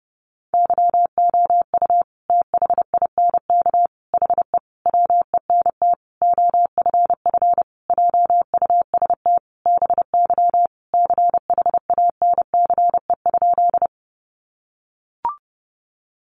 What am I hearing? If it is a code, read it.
Morse code, heard as a keyed tone, about 28 wpm, 700 Hz. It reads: YOU THINK HE WENT OFF JUST BY CHANCE?
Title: Sentences from Top 800 Words 30wpm